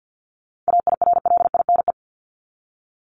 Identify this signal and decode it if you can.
Morse code, heard as a keyed tone, about 50 wpm, 700 Hz. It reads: AIRLINE
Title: Top 1501-1750 Words 50wpm